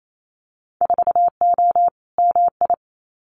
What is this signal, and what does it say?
Morse code, heard as a keyed tone, about 28 wpm, 700 Hz. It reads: 4O MI
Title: Winter Field Day Exchanges Encore 28wpm